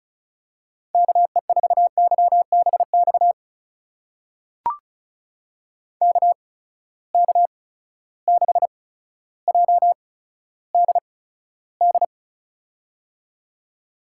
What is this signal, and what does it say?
Morse code, heard as a keyed tone, about 36 wpm, spaced as for 15 wpm, 700 Hz. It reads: KE4YBX KK6JDD
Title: USA Call Signs - 6 Characters - 8x Character Spacing - 35wpm